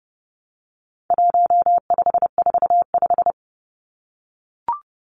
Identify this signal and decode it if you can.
Morse code, heard as a keyed tone, about 30 wpm, 700 Hz. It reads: 1545
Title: Numbers - 4-Digit 30wpm